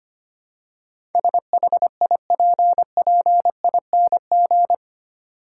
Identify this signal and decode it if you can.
Morse code, heard as a keyed tone, about 25 wpm, 700 Hz. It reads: SHIPPING
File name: 8 Letter Words - First 3 Letters Given - 25wpm